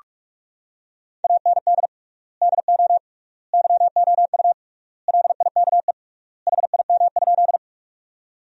Morse code, heard as a keyed tone, about 43 wpm, 700 Hz. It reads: AND DO YOU LIKE HIM?
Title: 3x Word-Spacing - Sentences from Top 100 Words - 45wpm